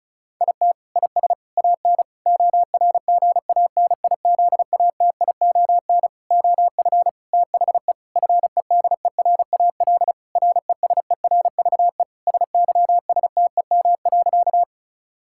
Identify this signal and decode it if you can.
Morse code, heard as a keyed tone, about 36 wpm, 700 Hz. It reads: IT IS AN ORGANIZATION OF THE FEDERAL RESERVE SYSTEM.
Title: Sentences from Top 1000 Words - Review Part 6 - 35wpm